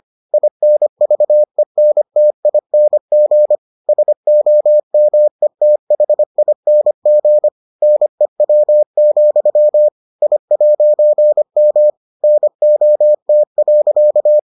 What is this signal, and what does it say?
Morse code, heard as a keyed tone, about 25 wpm, 600 Hz. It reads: INVENTING SOMETHING NEW, I'M NOT.